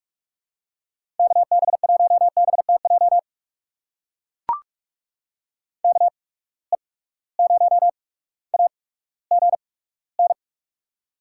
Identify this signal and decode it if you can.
Morse code, heard as a keyed tone, about 41 wpm, spaced as for 19 wpm, 700 Hz. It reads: KB1BTJ KE0AGN
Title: USA Call Signs - 6 Characters - 8x Character Spacing - 45wpm